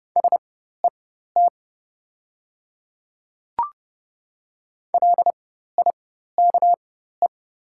Morse code, heard as a keyed tone, about 32 wpm, spaced as for 14 wpm, 690 Hz. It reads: SET LIKE
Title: 4x Character Spacing - Top 500 Words - 30wpm